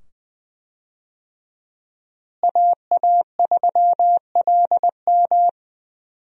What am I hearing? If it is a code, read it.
Morse code, heard as a keyed tone, about 20 wpm, 700 Hz. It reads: AA3LM
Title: USA Call Signs - 5 Characters - Repeated 2x - 20wpm